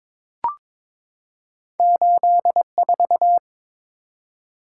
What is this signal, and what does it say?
Morse code, heard as a keyed tone, about 22 wpm, 700 Hz. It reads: 84
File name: practice / 22wpm / Two Letter-Number Combinations - 22wpm